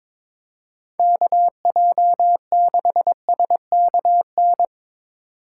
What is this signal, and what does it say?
Morse code, heard as a keyed tone, about 25 wpm, 700 Hz. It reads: KJ6SKN
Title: Speed Racing - USA 2x3 Call Signs 22wpm